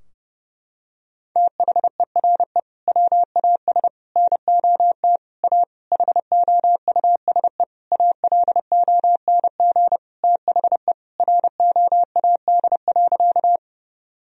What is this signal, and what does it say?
Morse code, heard as a keyed tone, about 32 wpm, 700 Hz. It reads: THERE WAS NOT A HOUSE ALONG THE ROAD.
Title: Sentences from Top 1000 Words - Review Part 2 - 30wpm